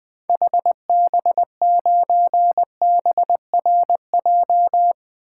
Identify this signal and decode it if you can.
Morse code, heard as a keyed tone, about 20 wpm, 700 Hz. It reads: HB9BRJ
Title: 2023 SOTA Activator Call Signs - 20wpm